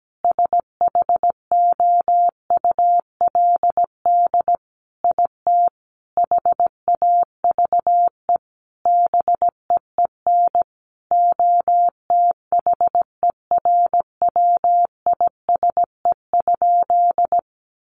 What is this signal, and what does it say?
Morse code, heard as a keyed tone, about 17 wpm, 700 Hz. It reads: SHOULD IT HAVE BEEN OTHERWISE?
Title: Sentences from Top 1200 Words 17wpm